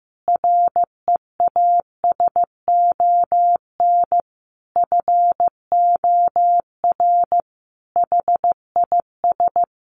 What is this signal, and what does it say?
Morse code, heard as a keyed tone, about 15 wpm, 700 Hz. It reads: REASON FOR HIS
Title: Sentences from Top 1300 Words 15wpm